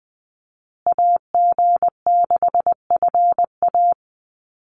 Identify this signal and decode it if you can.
Morse code, heard as a keyed tone, about 20 wpm, 700 Hz. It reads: AG6FA
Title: USA Call Signs Repeated 3x 20wpm